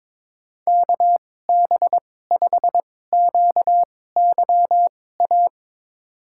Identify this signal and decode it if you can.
Morse code, heard as a keyed tone, about 22 wpm, spaced as for 16 wpm, 700 Hz. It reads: KB5QYA
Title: USA Call Signs - 2x Character Spacing - 22wpm